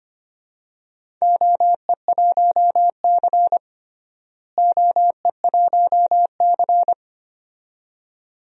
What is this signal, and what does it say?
Morse code, heard as a keyed tone, about 25 wpm, 700 Hz. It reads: OE1C OE1C
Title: DXCC-100 Call Signs - 4 Characters - Repeated 2x - 25wpm